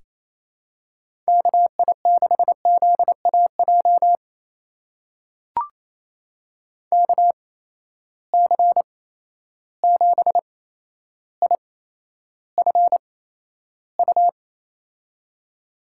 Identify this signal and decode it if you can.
Morse code, heard as a keyed tone, about 28 wpm, spaced as for 12 wpm, 700 Hz. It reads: KI6ZAJ KC7IFU
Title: USA Call Signs - 6 Characters - 8x Character Spacing - 28wpm